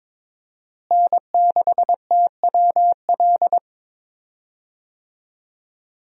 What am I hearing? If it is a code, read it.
Morse code, heard as a keyed tone, about 22 wpm, 700 Hz. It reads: N6TWL
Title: USA Call Signs - 5 Characters - Repeated 3x - 22wpm